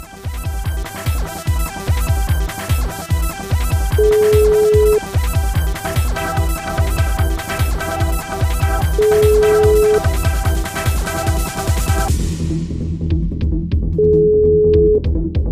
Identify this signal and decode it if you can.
Ringback tone (Europe, 425 Hz).